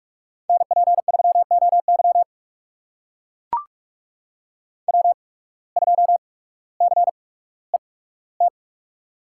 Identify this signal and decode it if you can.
Morse code, heard as a keyed tone, about 45 wpm, spaced as for 19 wpm, 700 Hz. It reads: NP3OY W2CET
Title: USA Call Signs - 8x Character Spacing - 45wpm